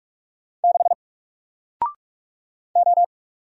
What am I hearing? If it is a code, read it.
Morse code, heard as a keyed tone, about 46 wpm, 700 Hz. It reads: 6 O
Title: Single Character (Letter:Number:Punctuation:Prosign) 45wpm